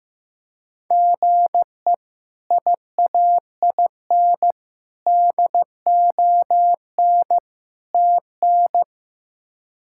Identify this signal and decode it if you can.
Morse code, heard as a keyed tone, about 15 wpm, 700 Hz. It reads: GE IAIN DON TN
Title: K1USN SST QSO Practice - 15wpm